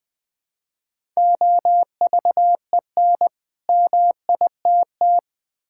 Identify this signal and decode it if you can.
Morse code, heard as a keyed tone, about 20 wpm, 700 Hz. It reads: OVEN MITT
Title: Easy Words with Hints - 20wpm